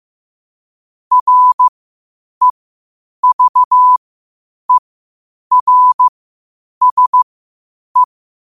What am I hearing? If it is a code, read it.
Morse code, heard as a keyed tone, about 15 wpm, spaced as for 8 wpm, 1000 Hz. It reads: REVERSE
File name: qotdfiles / 8wpm